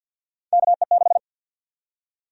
Morse code, heard as a keyed tone, about 53 wpm, 700 Hz. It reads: KE6
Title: USA Call Sign Prefixes - Repeated 2x - 50wpm